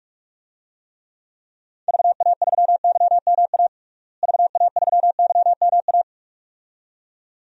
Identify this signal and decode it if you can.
Morse code, heard as a keyed tone, about 46 wpm, 700 Hz. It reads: VA3YMA VA3YMA
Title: DXCC-100 Call Signs - 6 Characters - Repeated 2x - 45wpm